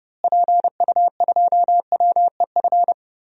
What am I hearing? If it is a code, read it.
Morse code, heard as a keyed tone, about 32 wpm, 700 Hz. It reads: PU2WEF
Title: DXCC-100 Call Signs - 6 Characters - Repeated 3x - 30wpm